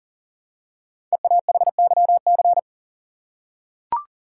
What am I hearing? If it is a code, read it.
Morse code, heard as a keyed tone, about 40 wpm, 700 Hz. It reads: EAHYC
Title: Code Groups - 5 Letters - 40wpm